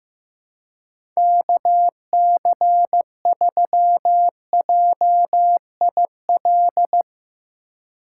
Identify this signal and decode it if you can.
Morse code, heard as a keyed tone, about 15 wpm, 700 Hz. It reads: KC3JIL
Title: USA Call Signs - 6 Characters - 2x Character Spacing - 15wpm